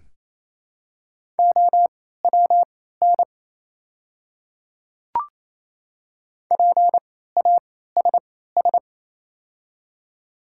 Morse code, heard as a keyed tone, about 28 wpm, spaced as for 16 wpm, 700 Hz. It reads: OWN PASS
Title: 3x Character Spacing - Top 500 Words - 28wpm